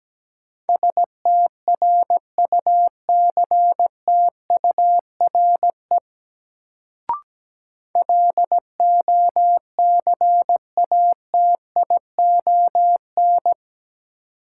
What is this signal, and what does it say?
Morse code, heard as a keyed tone, about 17 wpm, 700 Hz. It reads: STRUCTURE LOCATION
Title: Sentences from Top 1000 Words - Review Part 6 - New Word Drill - 17wpm